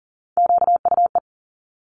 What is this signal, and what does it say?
Morse code, heard as a keyed tone, about 40 wpm, 700 Hz. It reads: QUE
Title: Multi-letter Phonemes Quick Fire 40wpm